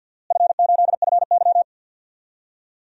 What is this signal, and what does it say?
Morse code, heard as a keyed tone, about 52 wpm, 700 Hz. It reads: F8FY
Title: DXCC-100 Call Signs - 4 Characters - Repeated 3x - 50wpm